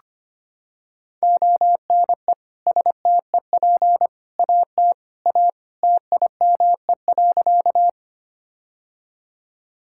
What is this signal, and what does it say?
Morse code, heard as a keyed tone, about 25 wpm, 700 Hz. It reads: ONE STEP AT A TIME.
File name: practice / 25wpm / Sentences with 3-Word Phrases - 25wpm